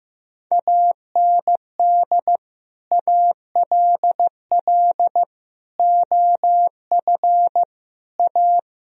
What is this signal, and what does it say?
Morse code, heard as a keyed tone, about 15 wpm, 700 Hz. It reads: AND ALL OF A
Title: Speed-Racing - Sets of 5 Words 15wpm